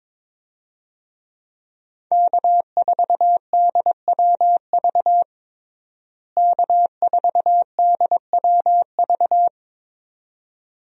Morse code, heard as a keyed tone, about 22 wpm, 700 Hz. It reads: K4DWV K4DWV
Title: USA Call Signs - 5 Characters - Repeated 2x - 22wpm